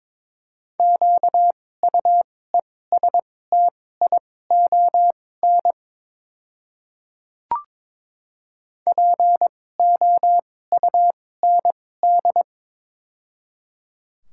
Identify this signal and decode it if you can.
Morse code, heard as a keyed tone, about 22 wpm, spaced as for 16 wpm, 700 Hz. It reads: QUESTION POUND
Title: 2x Character Spacing - Top 500 Words - 22wpm